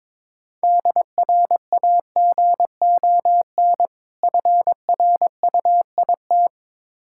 Morse code, heard as a keyed tone, about 21 wpm, 690 Hz. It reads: DRAGON FRUIT
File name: Difficult Words with Hints - 22wpm